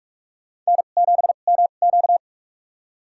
Morse code, heard as a keyed tone, about 46 wpm, spaced as for 32 wpm, 700 Hz. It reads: N7MQ
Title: USA Call Signs - 4 Characters - 2x Character Spacing - 45wpm